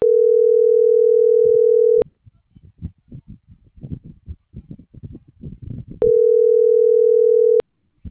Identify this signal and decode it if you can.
Ringback tone (North America), 440 + 480 Hz.